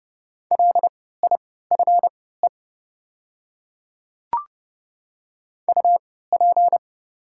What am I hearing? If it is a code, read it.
Morse code, heard as a keyed tone, about 30 wpm, spaced as for 17 wpm, 690 Hz. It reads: LIFE UP
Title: 3x Character Spacing - Top 300 Words - 30wpm